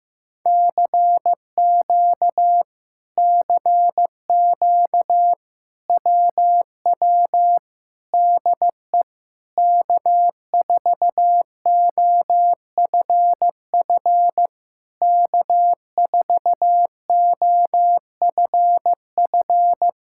Morse code, heard as a keyed tone, about 15 wpm, 700 Hz. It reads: CQ CQ WW DE K4OFF K4OFF
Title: Speed-Racing Common QSO Phrases Encore - 15wpm